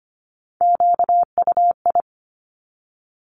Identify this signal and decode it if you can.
Morse code, heard as a keyed tone, about 25 wpm, 700 Hz. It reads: QUI
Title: Top Three Letter Combinations in Words 25wpm